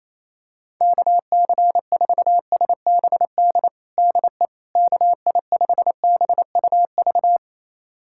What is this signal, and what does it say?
Morse code, heard as a keyed tone, about 28 wpm, 700 Hz. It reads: KC4SBD DE KI5BUV
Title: USA Callsign DE Callsign - 6 Characters - 28wpm